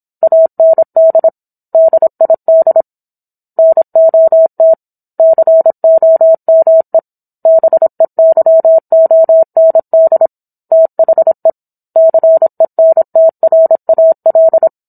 Morse code, heard as a keyed tone, about 27 wpm, 650 Hz. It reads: AND DID NOT COME BEYOND THE CENTRAL
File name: Book One / 17-Chapter 17 - 26 WPM